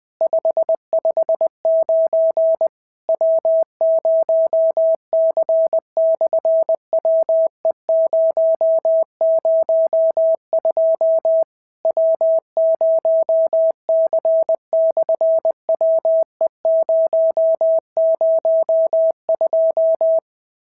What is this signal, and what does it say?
Morse code, heard as a keyed tone, about 19 wpm, 650 Hz. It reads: 559 W0C/WE002 W0C/WE002
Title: SOTA QSO Practice 20wpm